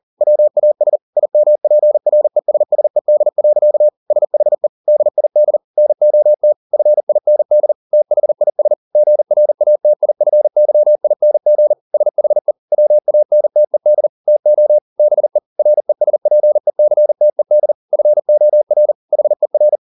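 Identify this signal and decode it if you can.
Morse code, heard as a keyed tone, about 40 wpm, 600 Hz. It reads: WAS IMPRESSED. SHE DID NOT FIND THIS GRATIFYING SHE WANTED TO BE RESPECTED FOR HER
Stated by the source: Morse code at 40 wpm, practice text, CW med Gnister.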